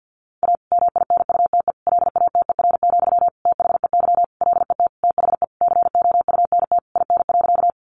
Morse code, heard as a keyed tone, about 50 wpm, 700 Hz. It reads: A MINUTE LATER, THEY LET THE COUNT IN.
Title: Sentences from Top 1500 Words 50wpm